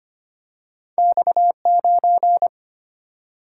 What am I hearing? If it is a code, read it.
Morse code, heard as a keyed tone, about 25 wpm, 700 Hz. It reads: X9
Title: Single-Letter Single-Number 25wpm